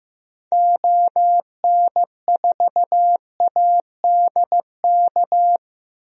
Morse code, heard as a keyed tone, about 14 wpm, 700 Hz. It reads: ON4ADK
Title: DXCC-100 Call Signs - 6 Characters - Repeated 3x - 15wpm